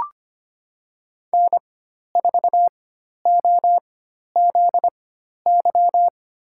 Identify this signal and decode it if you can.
Morse code, heard as a keyed tone, about 25 wpm, spaced as for 12 wpm, 700 Hz. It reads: N4OZY
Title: USA Call Signs - 5 Characters - 4x Character Spacing - 25wpm